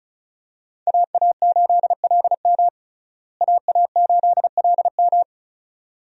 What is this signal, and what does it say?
Morse code, heard as a keyed tone, about 36 wpm, 700 Hz. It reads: AA8LM AA8LM
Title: USA Call Signs - 5 Characters - Repeated 2x - 35wpm